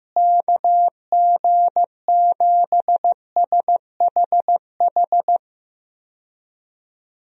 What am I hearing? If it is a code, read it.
Morse code, heard as a keyed tone, about 16 wpm, 700 Hz. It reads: KG7SHH
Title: USA Call Signs - 6 Characters - Repeated 3x - 15wpm